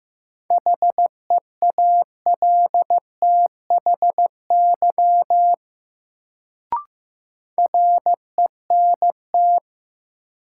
Morse code, heard as a keyed tone, about 15 wpm, 700 Hz. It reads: HEALTHY RENT
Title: Sentences from Top 1000 Words - Review Part 5 - New Word Drill - 15wpm